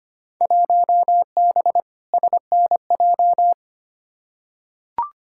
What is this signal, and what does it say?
Morse code, heard as a keyed tone, about 25 wpm, 700 Hz. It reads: 1B SNJ
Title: Field Day Exchanges Encore - 25wpm